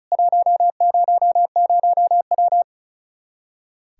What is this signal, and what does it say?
Morse code, heard as a keyed tone, about 36 wpm, 700 Hz. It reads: 100W
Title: Practice QSO - Exchange 2 - Info - 35wpm